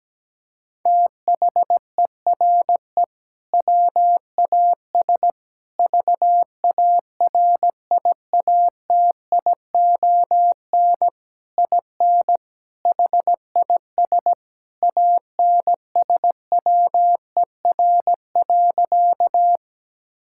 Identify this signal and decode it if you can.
Morse code, heard as a keyed tone, about 17 wpm, 700 Hz. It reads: THERE WAS VARIATION IN HIS ANSWER.
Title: Sentences from Top 1600 Words 17wpm